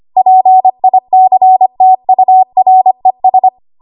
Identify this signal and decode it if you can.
Morse code, heard as a keyed tone, about 25 wpm, 750 Hz. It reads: PICTURES
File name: Archive / 25 WPM / 25WPM